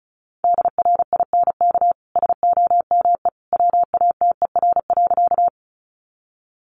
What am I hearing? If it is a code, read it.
Morse code, heard as a keyed tone, about 34 wpm, 700 Hz. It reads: DRINK SOME WATER.
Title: Sentences from Top 1500 Words 35wpm